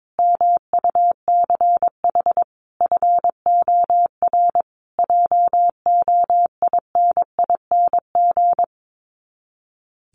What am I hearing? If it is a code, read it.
Morse code, heard as a keyed tone, about 22 wpm, 700 Hz. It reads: MUCH FOR JOINING
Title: Sets of 5 Words Encore 22wpm